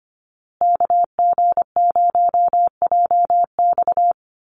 Morse code, heard as a keyed tone, about 25 wpm, 700 Hz. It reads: KG0JX
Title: USA Call Signs Repeated 3x 25wpm